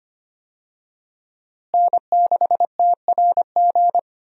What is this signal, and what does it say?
Morse code, heard as a keyed tone, about 25 wpm, 700 Hz. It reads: N6TRG
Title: USA Call Signs - 5 Characters - Repeated 3x - 25wpm